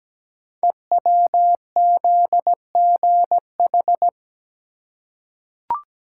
Morse code, heard as a keyed tone, about 17 wpm, 700 Hz. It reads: EWZGH
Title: Code Groups - 5 Letters - 17wpm